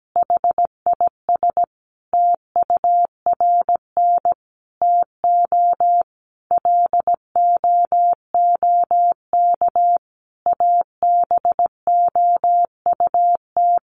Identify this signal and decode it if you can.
Morse code, heard as a keyed tone, about 17 wpm, 700 Hz. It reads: HIS TURN TO LOOK ABOUT
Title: Sentences from Top 200 Words 17wpm